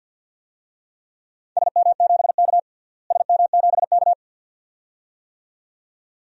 Morse code, heard as a keyed tone, about 51 wpm, 700 Hz. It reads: SM7K SM7K
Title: DXCC-100 Call Signs - 4 Characters - Repeated 2x - 50wpm